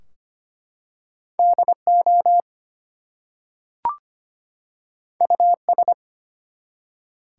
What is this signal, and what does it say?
Morse code, heard as a keyed tone, about 25 wpm, 700 Hz. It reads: DO US
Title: Two Letter Words - 25wpm